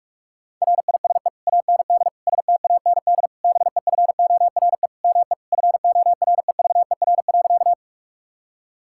Morse code, heard as a keyed tone, about 46 wpm, 700 Hz. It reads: RISE AND STAND BEFORE ME FOREVER.
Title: Sentences from Top 1600 Words 45wpm